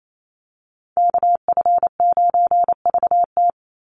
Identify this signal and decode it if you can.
Morse code, heard as a keyed tone, about 28 wpm, 700 Hz. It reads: KF9VT
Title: USA Call Signs 28wpm